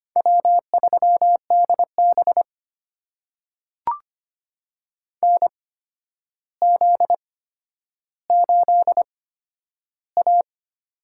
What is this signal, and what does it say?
Morse code, heard as a keyed tone, about 25 wpm, spaced as for 11 wpm, 700 Hz. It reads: W3DB NZ8A
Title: USA Call Signs - 8x Character Spacing - 25wpm